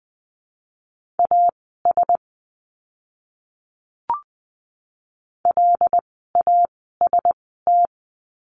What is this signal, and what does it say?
Morse code, heard as a keyed tone, about 20 wpm, spaced as for 14 wpm, 700 Hz. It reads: AS LAST
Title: 244 Lesson 39f - Letter z - Words 20wpm